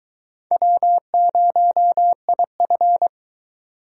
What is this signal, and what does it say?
Morse code, heard as a keyed tone, about 23 wpm, 700 Hz. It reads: W0IF